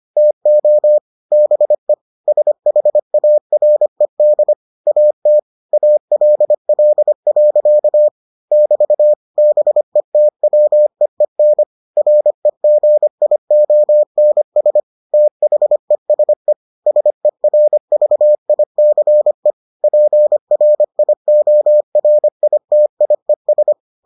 Morse code, heard as a keyed tone, about 25 wpm, 600 Hz. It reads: TO BE SHARED AT ALL. <BT> BETWEEN REGIONS THESE SERVICE PRIORITIES